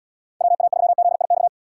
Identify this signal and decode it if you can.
Morse code, heard as a keyed tone, about 78 wpm, 700 Hz. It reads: WA3ZIX